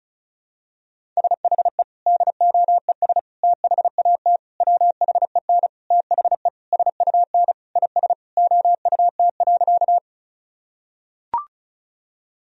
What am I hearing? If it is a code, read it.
Morse code, heard as a keyed tone, about 36 wpm, 700 Hz. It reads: SHE DOES THAT WHEN THE SUN IS OUT.
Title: Sentences from Top 1000 Words - Review Part 1 - 35wpm